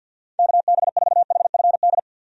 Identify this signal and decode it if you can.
Morse code, heard as a keyed tone, about 50 wpm, 700 Hz. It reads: KB4HFD
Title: USA Call Signs - 6 Characters - 50wpm